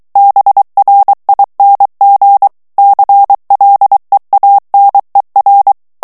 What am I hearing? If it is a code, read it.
Morse code, heard as a keyed tone, about 23 wpm, 800 Hz. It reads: BRING CLEANER
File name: morse mp3 / ANGLAIS texte clair 2 vitesse 22 mots minute